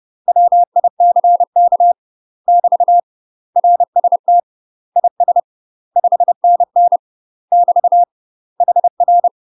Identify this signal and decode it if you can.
Morse code, heard as a keyed tone, about 30 wpm, 700 Hz. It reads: WICK <BT> RST IS 5NN <BT> HR
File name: UK MASTER QSOs TRAINING / 30WPM / UK MASTER QSOs 0047